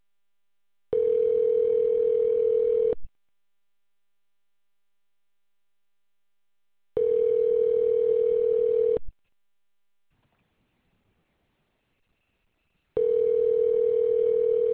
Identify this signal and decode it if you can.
Ringback tone (North America), 440 + 480 Hz.